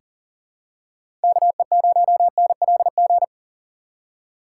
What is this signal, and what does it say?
Morse code, heard as a keyed tone, about 40 wpm, 700 Hz. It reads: KE0NLG
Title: USA Call Signs - 6 Characters - 2x Character Spacing - 40wpm